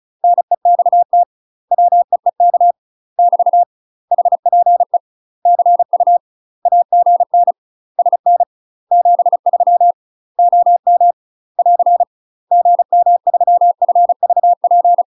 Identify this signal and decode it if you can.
Morse code, heard as a keyed tone, about 35 wpm, 700 Hz. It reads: NEXT WEEK <BT> HPE CU AGN SN 73 OM <AR> GM3FVP
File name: UK MASTER QSOs TRAINING / 35WPM / UK MASTER QSOs 0013